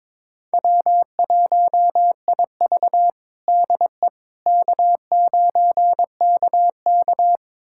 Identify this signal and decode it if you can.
Morse code, heard as a keyed tone, about 22 wpm, 700 Hz. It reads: W1IV DE K9KK
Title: USA Callsign DE Callsign - 4 Characters - 22wpm